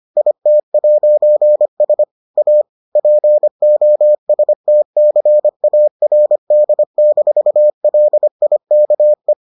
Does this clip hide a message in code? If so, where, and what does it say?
Yes: Morse code from 0.2 to 9.3 s, heard as a keyed tone, about 25 wpm, 600 Hz. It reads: IT'S A POSTCARD-LIKE